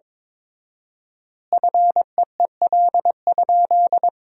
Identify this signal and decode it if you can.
Morse code, heard as a keyed tone, about 22 wpm, 700 Hz. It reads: FEEL?
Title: 4x Word-Spacing - Sentences from Top 400 Words - 22wpm